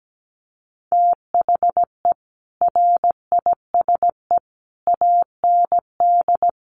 Morse code, heard as a keyed tone, about 17 wpm, 700 Hz. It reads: THE RISE AND
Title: Sets of 5 Words Encore2 17wpm